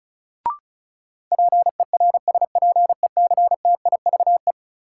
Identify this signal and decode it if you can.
Morse code, heard as a keyed tone, about 36 wpm, 700 Hz. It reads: PERSPECTIVE
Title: Sentences from Top 1200 Words - New Word Drill - 35wpm